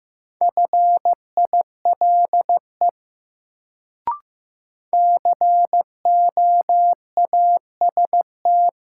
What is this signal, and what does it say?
Morse code, heard as a keyed tone, about 15 wpm, 700 Hz. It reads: FILE COAST
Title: Sentences from Top 1000 Words - Review Part 4 - New Word Drill - 15wpm